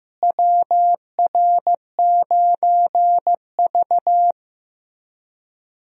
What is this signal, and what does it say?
Morse code, heard as a keyed tone, about 16 wpm, 700 Hz. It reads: WR9V
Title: USA Call Signs - 4 Characters - Repeated 3x - 15wpm